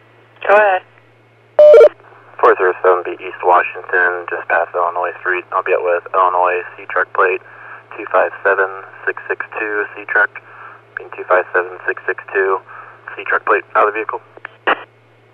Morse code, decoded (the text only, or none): none